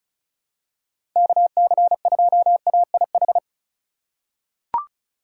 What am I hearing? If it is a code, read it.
Morse code, heard as a keyed tone, about 36 wpm, 700 Hz. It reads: KC2AIH
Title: USA Call Signs - 4x Character Spacing - 35wpm